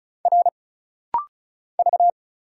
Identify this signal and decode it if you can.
Morse code, heard as a keyed tone, about 36 wpm, 700 Hz. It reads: R V
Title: Single Letter-Number Alt - Rapid-Fire 35wpm